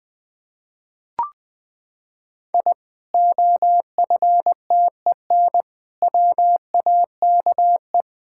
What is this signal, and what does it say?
Morse code, heard as a keyed tone, about 20 wpm, 700 Hz. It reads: I OFTEN WAKE
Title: Sentences with 8-Word Phrases - 20wpm